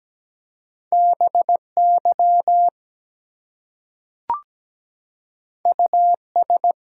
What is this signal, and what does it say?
Morse code, heard as a keyed tone, about 17 wpm, 700 Hz. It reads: BY US
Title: Two Letter Words - 17wpm